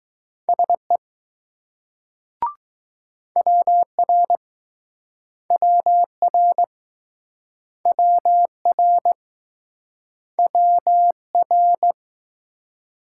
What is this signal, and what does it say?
Morse code, heard as a keyed tone, about 19 wpm, 700 Hz. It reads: SE WR WR WR WR